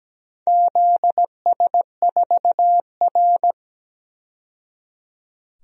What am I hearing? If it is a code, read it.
Morse code, heard as a keyed tone, about 17 wpm, 700 Hz. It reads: ZS4R
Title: DXCC-100 Call Signs - 4 Characters - 17wpm